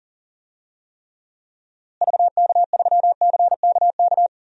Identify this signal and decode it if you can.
Morse code, heard as a keyed tone, about 40 wpm, 700 Hz. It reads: VK3CKK
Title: DXCC-100 Call Signs - 6 Characters - Repeated 3x - 40wpm